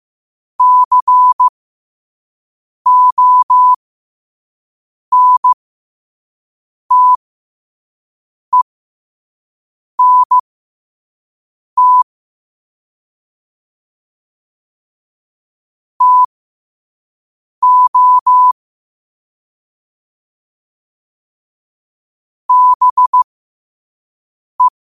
Morse code, heard as a keyed tone, about 15 wpm, spaced as for 5 wpm, 1000 Hz. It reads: CONTENT TO BE
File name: qotdfiles / 5wpm